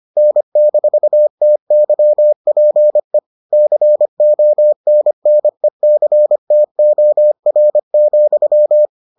Morse code, heard as a keyed tone, about 25 wpm, 600 Hz. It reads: N-TYPE CONNECTOR,